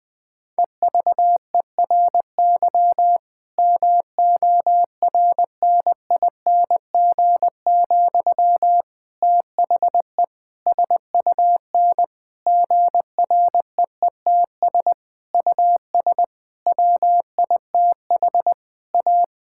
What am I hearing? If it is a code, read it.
Morse code, heard as a keyed tone, about 19 wpm, 700 Hz. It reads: EVERY MORNING, THE SUN GREETS US WITH A
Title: Difficult Sentences with Hints - 20wpm